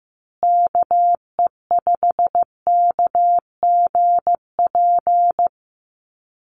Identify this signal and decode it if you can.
Morse code, heard as a keyed tone, about 15 wpm, 700 Hz. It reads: KE5KGP
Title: USA Call Signs Repeated 3x 15wpm